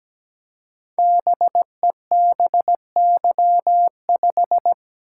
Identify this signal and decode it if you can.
Morse code, heard as a keyed tone, about 17 wpm, 700 Hz. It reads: BEBY5
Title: Code Groups - 5 Letter:Numbers - 17wpm